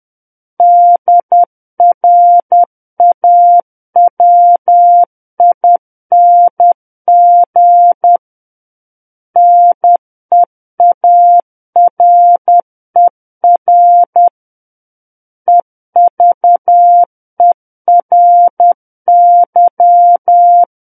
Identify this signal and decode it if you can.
Morse code, heard as a keyed tone, about 10 wpm, 700 Hz. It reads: DRAWING NEARER EVERY
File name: Book One / 01-Chapter 1 - 10 WPM